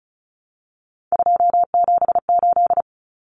Morse code, heard as a keyed tone, about 34 wpm, 700 Hz. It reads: 278
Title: Common Cut Numbers - 3-Digit 35wpm